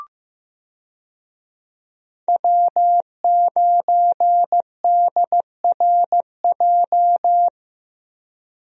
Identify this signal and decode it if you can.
Morse code, heard as a keyed tone, about 15 wpm, 700 Hz. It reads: W9DRJ
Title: USA Call Signs - 5 Characters - Repeated 3x - 15wpm